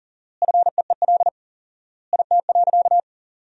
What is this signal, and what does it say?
Morse code, heard as a keyed tone, about 40 wpm, 700 Hz. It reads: FEEL IT.
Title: 4x Word-Spacing - Sentences from Top 400 Words - 40wpm